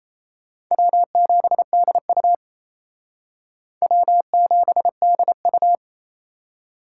Morse code, heard as a keyed tone, about 35 wpm, 700 Hz. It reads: W7DU W7DU